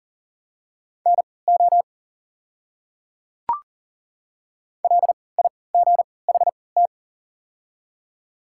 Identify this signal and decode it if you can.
Morse code, heard as a keyed tone, about 40 wpm, spaced as for 23 wpm, 700 Hz. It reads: NO LIGHT
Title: 3x Character Spacing - Top 200 Words - 40wpm